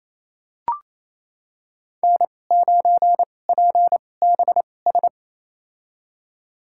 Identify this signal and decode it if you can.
Morse code, heard as a keyed tone, about 28 wpm, spaced as for 20 wpm, 700 Hz. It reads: N9PBS
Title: USA Call Signs - 5 Characters - 2x Character Spacing - 28wpm